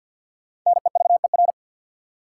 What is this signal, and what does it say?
Morse code, heard as a keyed tone, about 53 wpm, 700 Hz. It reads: NEVER
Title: Sentences from Top 300 Words - New Word Drill - 50wpm